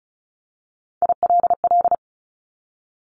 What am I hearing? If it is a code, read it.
Morse code, heard as a keyed tone, about 36 wpm, 690 Hz. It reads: ILL
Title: Top Three Letter Combinations in Words 35wpm